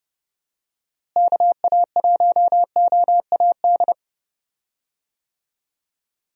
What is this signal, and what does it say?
Morse code, heard as a keyed tone, about 30 wpm, 700 Hz. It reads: KA1OAD
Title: USA Call Signs - 6 Characters - Repeated 3x - 30wpm